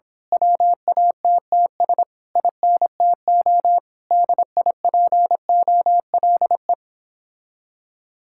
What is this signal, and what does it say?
Morse code, heard as a keyed tone, about 26 wpm, 700 Hz. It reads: WATTS INTO DIPOLE
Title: Speed-Racing Common QSO Phrases - 22wpm